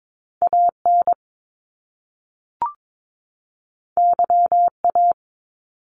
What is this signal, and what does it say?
Morse code, heard as a keyed tone, about 22 wpm, 700 Hz. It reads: AN YA
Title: Top Two Letter Combinations in Words 22wpm